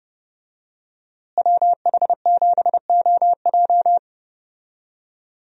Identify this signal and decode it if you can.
Morse code, heard as a keyed tone, about 32 wpm, 700 Hz. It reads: WH7OJ
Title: USA Call Signs - 5 Characters - Repeated 3x - 30wpm